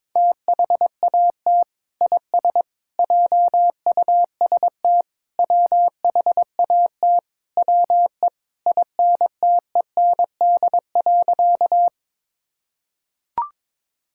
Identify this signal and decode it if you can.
Morse code, heard as a keyed tone, about 22 wpm, 700 Hz. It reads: THAT IS JUST WHAT WE INTEND.
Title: Sentences from Top 1600 Words 22wpm